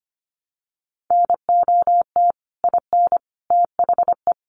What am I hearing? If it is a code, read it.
Morse code, heard as a keyed tone, about 25 wpm, 700 Hz. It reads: NOT IN THE
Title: Sentences from Top 700 Words 25wpm